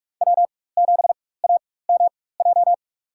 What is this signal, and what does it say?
Morse code, heard as a keyed tone, about 43 wpm, spaced as for 21 wpm, 700 Hz. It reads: W7AMJ
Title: USA Call Signs - 5 Characters - 4x Character Spacing - 45wpm